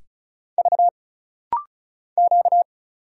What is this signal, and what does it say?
Morse code, heard as a keyed tone, about 36 wpm, 700 Hz. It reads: V Q
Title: Single Letters - Rapid-Fire 35wpm